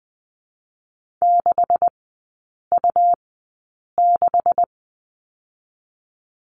Morse code, heard as a keyed tone, about 20 wpm, 700 Hz. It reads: 6 U 6
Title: 127 Lesson 21e - Number 6 - Three Characters 20wpm